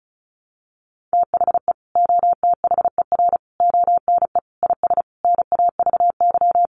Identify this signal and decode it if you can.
Morse code, heard as a keyed tone, about 34 wpm, 700 Hz. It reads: THE OTHER ONE IS NAVY
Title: Additional Easy Phrases 35wpm